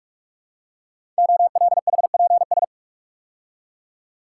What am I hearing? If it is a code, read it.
Morse code, heard as a keyed tone, about 44 wpm, 700 Hz. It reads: OLHPS
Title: Code Groups - 5 Letters - 45wpm